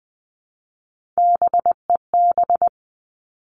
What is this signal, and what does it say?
Morse code, heard as a keyed tone, about 20 wpm, 700 Hz. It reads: BEB
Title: Three Challenging Letter-Number Combinations 20wpm